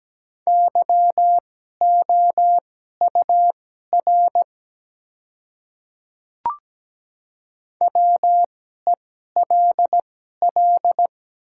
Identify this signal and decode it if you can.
Morse code, heard as a keyed tone, about 17 wpm, spaced as for 12 wpm, 700 Hz. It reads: YOUR WELL
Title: 2x Character Spacing - Top 200 Words - 17wpm